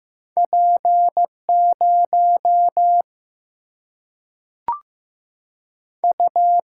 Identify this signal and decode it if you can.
Morse code, heard as a keyed tone, about 15 wpm, 700 Hz. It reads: P 0 U
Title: Single-Letter Single-Number 15wpm